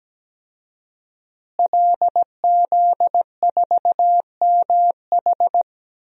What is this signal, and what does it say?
Morse code, heard as a keyed tone, about 17 wpm, 700 Hz. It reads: LZ4MH
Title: DXCC-100 Call Signs - 5 Characters - Repeated 3x - 17wpm